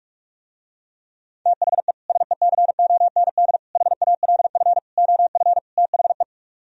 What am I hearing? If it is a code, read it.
Morse code, heard as a keyed tone, about 45 wpm, 700 Hz. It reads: THE SECOND HALF OF THE